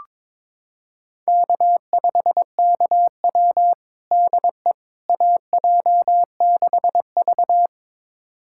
Morse code, heard as a keyed tone, about 22 wpm, 700 Hz. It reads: K5KW DE AJ6V
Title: USA Callsign DE Callsign - 4 Characters - 22wpm